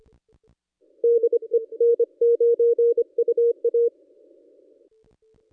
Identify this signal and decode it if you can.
Morse code, heard as a keyed tone, about 27 wpm, 470 Hz. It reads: DE N9UA
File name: contests / ss-cw-k5tr-op-n5rz -1700